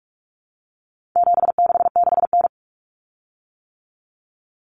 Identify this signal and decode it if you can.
Morse code, heard as a keyed tone, about 44 wpm, 700 Hz. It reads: 766N
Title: Common Cut Numbers - 4-Digit 45wpm